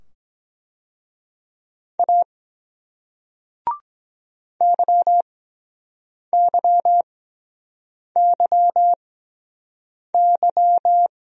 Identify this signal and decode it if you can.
Morse code, heard as a keyed tone, about 23 wpm, spaced as for 7 wpm, 700 Hz. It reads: A YYYY